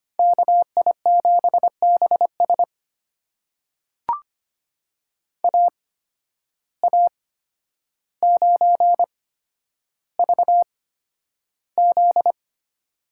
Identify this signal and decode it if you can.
Morse code, heard as a keyed tone, about 25 wpm, spaced as for 11 wpm, 700 Hz. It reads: KI7BS AA9VZ
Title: USA Call Signs - 5 Characters - 8x Character Spacing - 25wpm